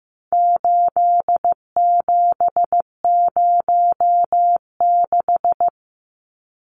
Morse code, heard as a keyed tone, about 15 wpm, 700 Hz. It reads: 8706
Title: Numbers - 4-Digit 15wpm